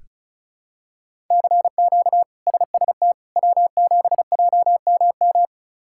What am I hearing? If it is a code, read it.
Morse code, heard as a keyed tone, about 34 wpm, 700 Hz. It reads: CQ SST W7JMM
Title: K1USN SST QSO Practice - 35wpm